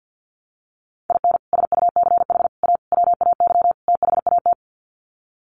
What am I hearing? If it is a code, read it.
Morse code, heard as a keyed tone, about 50 wpm, 700 Hz. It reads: IN SUCH A WAY THAT
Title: Sets of 5 Words 50wpm